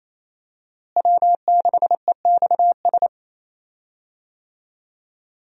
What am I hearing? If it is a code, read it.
Morse code, heard as a keyed tone, about 28 wpm, 700 Hz. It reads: W6EXS
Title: USA Call Signs - 5 Characters - Repeated 3x - 28wpm